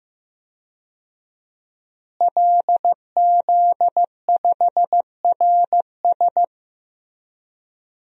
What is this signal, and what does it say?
Morse code, heard as a keyed tone, about 16 wpm, 700 Hz. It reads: LZ5RS
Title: DXCC-100 Call Signs - 5 Characters - Repeated 2x - 15wpm